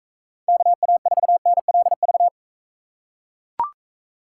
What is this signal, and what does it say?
Morse code, heard as a keyed tone, about 44 wpm, 700 Hz. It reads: KA4NLV